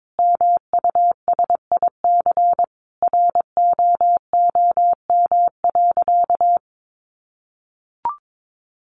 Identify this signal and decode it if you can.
Morse code, heard as a keyed tone, about 22 wpm, 700 Hz. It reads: MUSIC ROOM.
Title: Sentences from Top 700 Words 22wpm